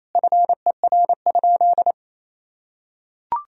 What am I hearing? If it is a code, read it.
Morse code, heard as a keyed tone, about 28 wpm, 700 Hz. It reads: FER?
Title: Sentences from Top 400 Words and Abbreviations 28wpm